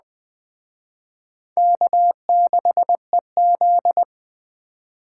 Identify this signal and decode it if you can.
Morse code, heard as a keyed tone, about 21 wpm, 700 Hz. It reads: K6EZ